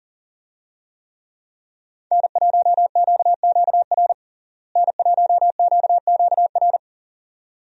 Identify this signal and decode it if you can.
Morse code, heard as a keyed tone, about 40 wpm, 700 Hz. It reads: N1QQR N1QQR
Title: USA Call Signs - 5 Characters - Repeated 2x - 40wpm